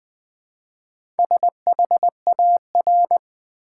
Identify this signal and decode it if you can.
Morse code, heard as a keyed tone, about 20 wpm, 700 Hz. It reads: SHAR
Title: Build A Word - 1 Letter At A Time - 5-Letter Words - 20wpm